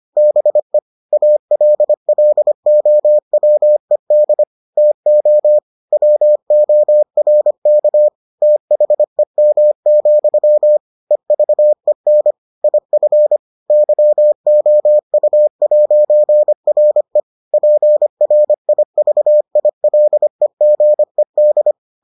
Morse code, heard as a keyed tone, about 25 wpm, 600 Hz. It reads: BE ALLOWED TO WORK THEM, EVEN IF YOU'RE PRIVILEGED